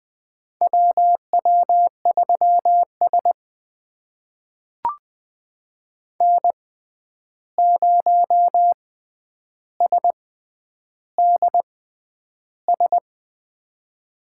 Morse code, heard as a keyed tone, about 20 wpm, spaced as for 10 wpm, 700 Hz. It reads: WW3S N0SDS
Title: USA Call Signs - 6x Character Spacing - 20wpm